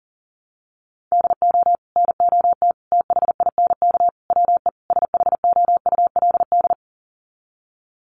Morse code, heard as a keyed tone, about 40 wpm, 700 Hz. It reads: DO NOT THINK WE SHOULD
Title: Sets of 5 Words Encore 40wpm